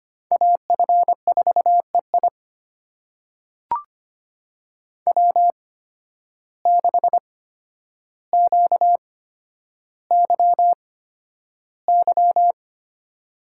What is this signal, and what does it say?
Morse code, heard as a keyed tone, about 25 wpm, spaced as for 11 wpm, 700 Hz. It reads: AF4EI W6QYY
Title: USA Call Signs - 5 Characters - 8x Character Spacing - 25wpm